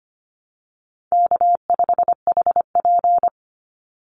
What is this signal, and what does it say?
Morse code, heard as a keyed tone, about 25 wpm, 700 Hz. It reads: K5HP
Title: Easy Call Signs Repeated 3x - 25wpm